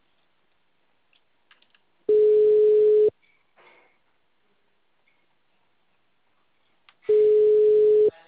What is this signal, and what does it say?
Ringback tone (Europe, 425 Hz).
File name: out- -2039- - - .50631